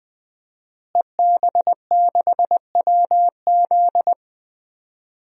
Morse code, heard as a keyed tone, about 20 wpm, 700 Hz. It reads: EB6WZ
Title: DXCC-100 Call Signs - 5 Characters - Repeated 3x - 20wpm